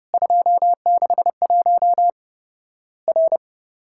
Morse code, heard as a keyed tone, about 30 wpm, 700 Hz. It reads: 261 R
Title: Worked All Europe DX Contest - QTC Exchange - 30wpm